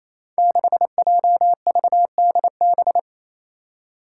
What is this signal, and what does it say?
Morse code, heard as a keyed tone, about 28 wpm, 700 Hz. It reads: 6JVDB
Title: Five Challenging Letter-Number Combinations 28wpm